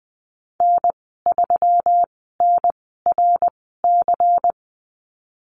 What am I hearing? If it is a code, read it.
Morse code, heard as a keyed tone, about 20 wpm, spaced as for 15 wpm, 700 Hz. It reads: N3NRC
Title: 122 Lesson 20g - Number 3 - Call Signs 20wpm